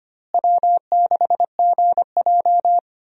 Morse code, heard as a keyed tone, about 25 wpm, 700 Hz. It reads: W6GJ
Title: USA Call Signs - 4 Characters - Repeated 2x - 25wpm